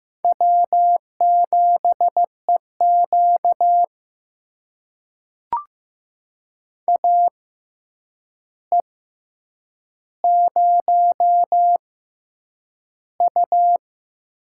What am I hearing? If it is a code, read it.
Morse code, heard as a keyed tone, about 15 wpm, spaced as for 8 wpm, 700 Hz. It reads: W7EQ AE0U
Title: USA Call Signs - 4 Characters - 6x Character Spacing - 15wpm